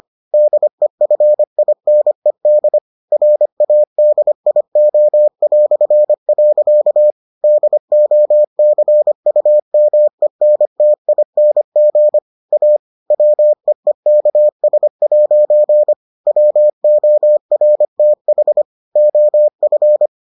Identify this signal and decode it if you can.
Morse code, heard as a keyed tone, about 25 wpm, 600 Hz. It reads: DEFINED RADIO". DOCUMENTING A WEEKS' WORTH OF